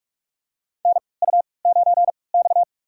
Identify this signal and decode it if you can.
Morse code, heard as a keyed tone, about 46 wpm, spaced as for 25 wpm, 700 Hz. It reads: NU9X
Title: USA Call Signs - 4 Characters - 3x Character Spacing - 45wpm